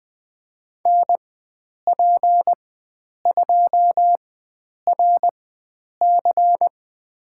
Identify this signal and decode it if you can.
Morse code, heard as a keyed tone, about 20 wpm, spaced as for 9 wpm, 700 Hz. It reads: NP2RC
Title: USA Call Signs - 5 Characters - 4x Character Spacing - 20wpm